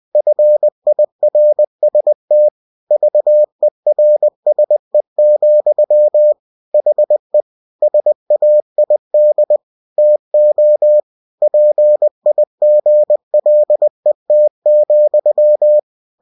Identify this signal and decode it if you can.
Morse code, heard as a keyed tone, about 20 wpm, 600 Hz. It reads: FIRST VERSE, HE SAID TO PIGLET,
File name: winnie-20wpm